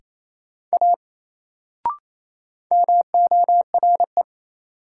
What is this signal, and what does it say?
Morse code, heard as a keyed tone, about 28 wpm, 700 Hz. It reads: A MORE
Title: Sentences from Top 100 Words - New Word Drill - 28wpm